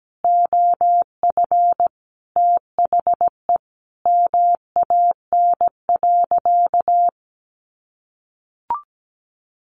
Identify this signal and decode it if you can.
Morse code, heard as a keyed tone, about 17 wpm, 700 Hz. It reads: OF THE MAN.
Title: Sentences from Top 1500 Words 17wpm